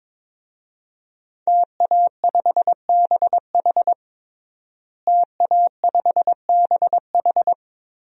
Morse code, heard as a keyed tone, about 22 wpm, 700 Hz. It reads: TA5BH TA5BH
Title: DXCC-100 Call Signs - 5 Characters - Repeated 2x - 22wpm